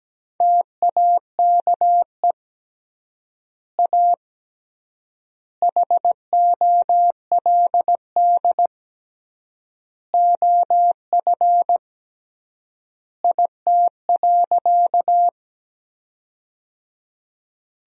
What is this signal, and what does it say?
Morse code, heard as a keyed tone, about 17 wpm, 700 Hz. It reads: TAKE A HOLD OF IT.
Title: 3x Word-Spacing - Sentences from Top 400 Words - 17wpm